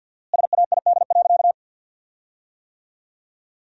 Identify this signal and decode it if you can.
Morse code, heard as a keyed tone, about 50 wpm, 700 Hz. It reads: SAID.
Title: 4x Word-Spacing - Sentences from Top 500 Words - 50wpm